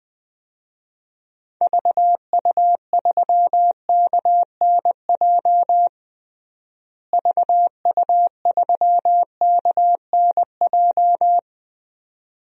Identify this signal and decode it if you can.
Morse code, heard as a keyed tone, about 20 wpm, 700 Hz. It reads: VU3KNJ VU3KNJ
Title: DXCC-100 Call Signs - 6 Characters - Repeated 2x - 20wpm